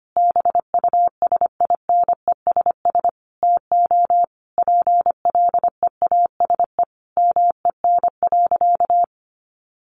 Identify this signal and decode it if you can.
Morse code, heard as a keyed tone, about 25 wpm, 700 Hz. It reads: BUSINESS TO PLEASE MEN.
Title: Sentences from Top 1500 Words 25wpm